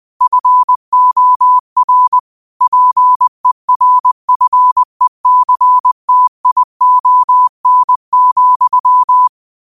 Morse code, heard as a keyed tone, about 20 wpm, 1000 Hz. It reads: FOR PERFECTION,